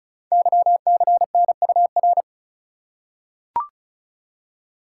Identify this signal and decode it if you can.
Morse code, heard as a keyed tone, about 36 wpm, 690 Hz. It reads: YCNUR
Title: Code Groups - 5 Letter:Numbers - 35wpm